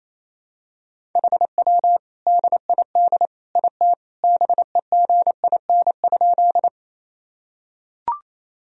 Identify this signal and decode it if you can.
Morse code, heard as a keyed tone, about 28 wpm, 700 Hz. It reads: HW DID IT BEGIN?
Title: Sentences from Top 300 Words and Abbreviations 28wpm